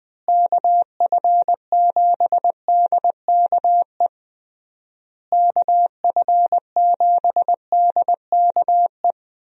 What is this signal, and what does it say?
Morse code, heard as a keyed tone, about 20 wpm, 700 Hz. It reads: KF7DKE KF7DKE
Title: USA Call Signs - 6 Characters - Repeated 3x - 20wpm